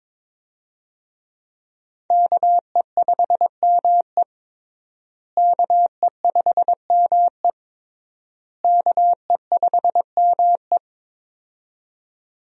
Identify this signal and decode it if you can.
Morse code, heard as a keyed tone, about 22 wpm, 700 Hz. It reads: KE5ME KE5ME KE5ME
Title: USA Call Signs - 5 Characters - Repeated 3x - 22wpm